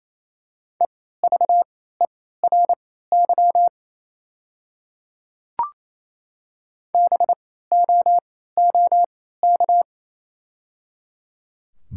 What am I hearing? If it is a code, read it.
Morse code, heard as a keyed tone, about 28 wpm, spaced as for 16 wpm, 700 Hz. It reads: EVERY BOOK
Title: 3x Character Spacing - Top 300 Words - 28wpm